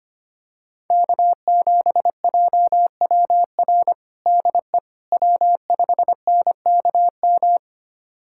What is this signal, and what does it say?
Morse code, heard as a keyed tone, about 25 wpm, 700 Hz. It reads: K7JWR DE W5NKM
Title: USA Callsign DE Callsign 25wpm